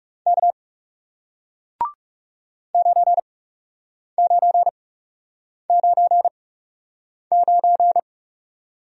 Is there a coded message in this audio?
Yes: Morse code from 0.3 to 8.0 s, heard as a keyed tone, about 40 wpm, spaced as for 8 wpm, 700 Hz. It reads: K 9999